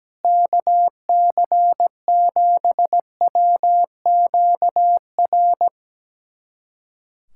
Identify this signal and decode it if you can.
Morse code, heard as a keyed tone, about 17 wpm, 700 Hz. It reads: KC7WQR
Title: USA Callsign DE Callsign 17wpm